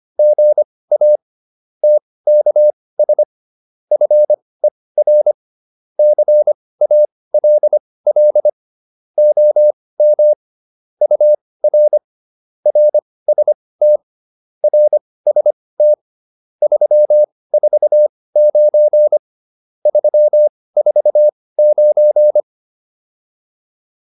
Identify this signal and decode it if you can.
Morse code, heard as a keyed tone, about 25 wpm, spaced as for 18 wpm, 600 Hz. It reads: GA TKS FER CALL OM UR RST RST 349 349
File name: QSO408 18